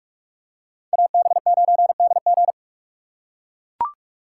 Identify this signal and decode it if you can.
Morse code, heard as a keyed tone, about 43 wpm, 700 Hz. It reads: AB9DG